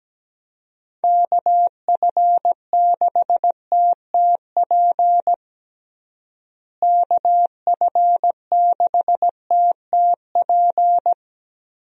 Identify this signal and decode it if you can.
Morse code, heard as a keyed tone, about 17 wpm, 700 Hz. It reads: KF6TTP KF6TTP
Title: USA Call Signs - 6 Characters - Repeated 3x - 17wpm